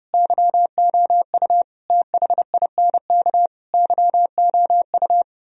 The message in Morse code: YOU THINK YOU

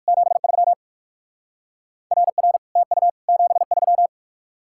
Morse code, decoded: 73 RR TU 73

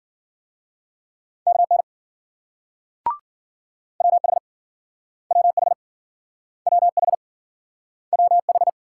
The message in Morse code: KN WH WH WH WH

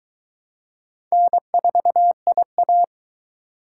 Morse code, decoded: N4IA

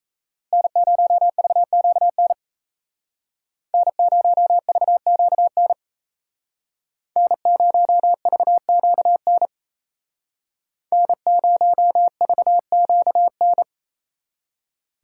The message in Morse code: N0VQN N0VQN N0VQN N0VQN

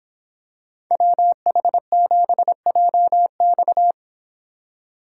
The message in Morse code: WH7JX

